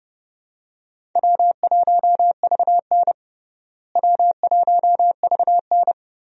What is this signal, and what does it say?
Morse code, heard as a keyed tone, about 30 wpm, 700 Hz. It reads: W1VN W1VN
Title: USA Call Signs - 4 Characters - Repeated 2x - 30wpm